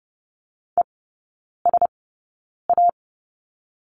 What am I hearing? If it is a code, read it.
Morse code, heard as a keyed tone, about 30 wpm, spaced as for 9 wpm, 690 Hz. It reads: ESA
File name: Lesson 16e - Letter c - Three Characters 30wpm